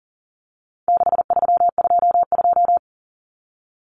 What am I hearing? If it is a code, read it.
Morse code, heard as a keyed tone, about 40 wpm, 700 Hz. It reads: 6322
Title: Numbers - 4-Digit 40wpm